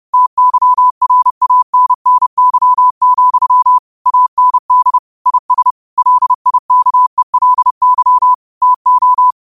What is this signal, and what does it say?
Morse code, heard as a keyed tone, about 30 wpm, 1000 Hz. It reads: TYRANNY, AND IS LIKELY TO
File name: qotdfiles / 30wpm